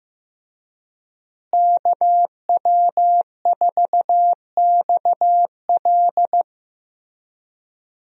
Morse code, heard as a keyed tone, about 15 wpm, 700 Hz. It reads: KW4XL
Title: USA Call Signs - 5 Characters - Repeated 3x - 15wpm